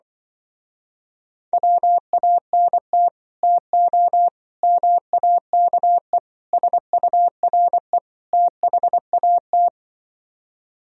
Morse code, heard as a keyed tone, about 24 wpm, 700 Hz. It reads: WANT TO MAKE SURE THAT
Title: Speed-Racing - Sets of 5 Words 20wpm